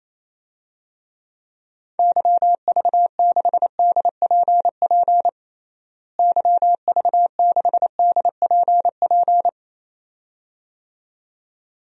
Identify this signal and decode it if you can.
Morse code, heard as a keyed tone, about 28 wpm, 700 Hz. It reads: YV6DPP YV6DPP
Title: DXCC-100 Call Signs - 6 Characters - Repeated 2x - 28wpm